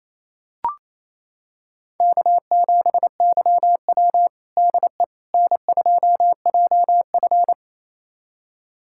Morse code, heard as a keyed tone, about 28 wpm, 700 Hz. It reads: K7YW DE N2JF
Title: USA Callsign DE Callsign - 4 Characters - 28wpm